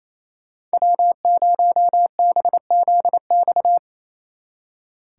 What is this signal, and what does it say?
Morse code, heard as a keyed tone, about 37 wpm, 700 Hz. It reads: W0BZX